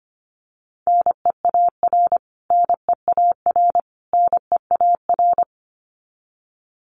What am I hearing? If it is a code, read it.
Morse code, heard as a keyed tone, about 25 wpm, 700 Hz. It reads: NEAR NEAR NEAR
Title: Top 200 Words Repeated 25wpm